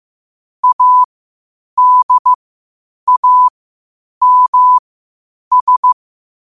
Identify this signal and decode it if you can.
Morse code, heard as a keyed tone, about 15 wpm, spaced as for 8 wpm, 1000 Hz. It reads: ADAMS
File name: qotdfiles / 8wpm